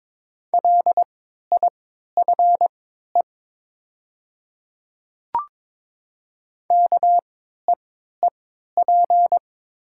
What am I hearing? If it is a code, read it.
Morse code, heard as a keyed tone, about 22 wpm, spaced as for 12 wpm, 700 Hz. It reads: LIFE KEEP
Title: 3x Character Spacing - Top 300 Words - 22wpm